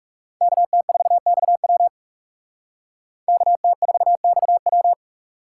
Morse code, heard as a keyed tone, about 38 wpm, 700 Hz. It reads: KT4XW KT4XW